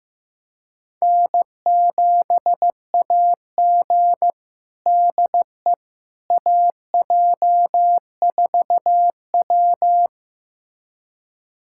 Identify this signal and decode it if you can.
Morse code, heard as a keyed tone, about 16 wpm, 700 Hz. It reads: N7AG DE AJ4W
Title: USA Callsign DE Callsign - 4 Characters - 15wpm